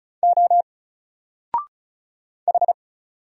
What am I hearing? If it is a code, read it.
Morse code, heard as a keyed tone, about 33 wpm, 700 Hz. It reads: O H
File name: Single-Letter-Number-35wpm